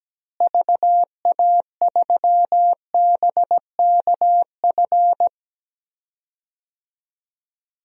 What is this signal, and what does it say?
Morse code, heard as a keyed tone, about 17 wpm, 700 Hz. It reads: VA3BKF
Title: DXCC-100 Call Signs - 6 Characters - Repeated 3x - 17wpm